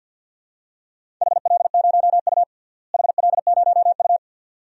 Morse code, heard as a keyed tone, about 48 wpm, 700 Hz. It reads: HL0U HL0U
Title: DXCC-100 Call Signs - 4 Characters - Repeated 2x - 50wpm